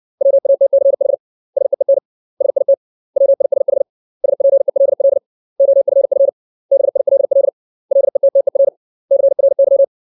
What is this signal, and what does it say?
Morse code, heard as a keyed tone, about 60 wpm, 550 Hz. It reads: WATCH SEEN SIT WISH SPELL OFF BILL LETTER CRY
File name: 632 words / 632 Words 60WPM